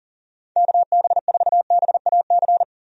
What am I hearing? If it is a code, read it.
Morse code, heard as a keyed tone, about 38 wpm, 700 Hz. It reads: KB4BAC
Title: USA Call Signs - 6 Characters - 40wpm